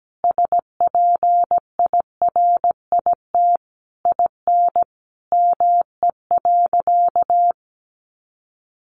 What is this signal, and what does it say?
Morse code, heard as a keyed tone, about 17 wpm, 700 Hz. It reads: SPIRIT IN ME.
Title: Sentences from Top 1400 Words 17wpm